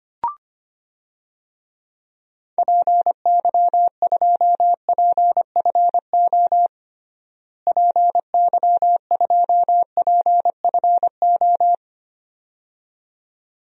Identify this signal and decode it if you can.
Morse code, heard as a keyed tone, about 25 wpm, 700 Hz. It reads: PY2PFO PY2PFO
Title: DXCC-100 Call Signs - 6 Characters - Repeated 2x - 25wpm